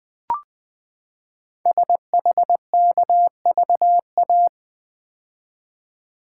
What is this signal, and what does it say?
Morse code, heard as a keyed tone, about 20 wpm, 700 Hz. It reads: SHKVA
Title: Code Groups - 5 Letters - 20wpm